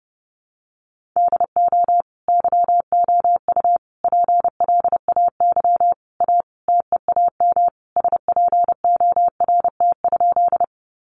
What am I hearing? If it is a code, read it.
Morse code, heard as a keyed tone, about 30 wpm, 700 Hz. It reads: DO YOU PLAY A TEAM SPORT?
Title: Sentences from Top 1500 Words 30wpm